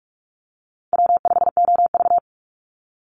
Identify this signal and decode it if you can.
Morse code, heard as a keyed tone, about 44 wpm, 700 Hz. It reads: W5OV
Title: Easy Call Signs - 45wpm